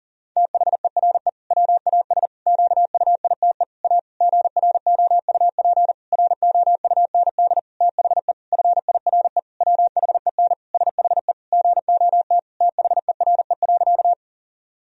Morse code, heard as a keyed tone, about 40 wpm, 700 Hz. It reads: THERE WAS QUITE A GROUP ROUND THE FIRE WHEN SHE GOT THERE.
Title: Sentences from Top 1000 Words - Review Part 1 - 40wpm